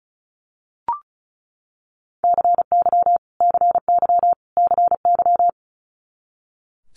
Morse code, heard as a keyed tone, about 36 wpm, 700 Hz. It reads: CY CY CY
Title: Multi-letter Phonemes Repeated 3x 35wpm